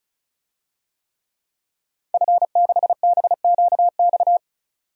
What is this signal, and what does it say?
Morse code, heard as a keyed tone, about 34 wpm, 700 Hz. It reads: F6BQX
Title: DXCC-100 Call Signs - 5 Characters - 35wpm